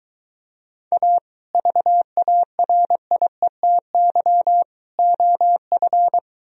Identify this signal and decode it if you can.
Morse code, heard as a keyed tone, about 23 wpm, 700 Hz. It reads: A VARIETY OF